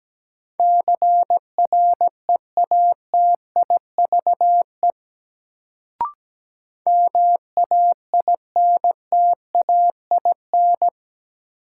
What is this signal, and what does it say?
Morse code, heard as a keyed tone, about 17 wpm, 700 Hz. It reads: CREATIVE MAINTAIN
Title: Sentences from Top 1000 Words - Review Part 5 - New Word Drill - 17wpm